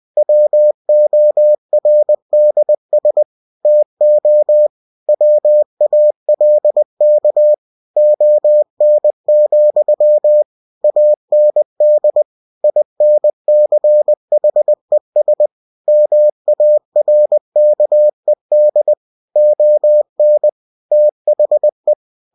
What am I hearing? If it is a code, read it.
Morse code, heard as a keyed tone, about 20 wpm, 600 Hz. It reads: WORDS TO WALK ON, AND INCHES MARKED ON THE
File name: cw-books / winnie-the-pooh / winnie-20wpm-page-0109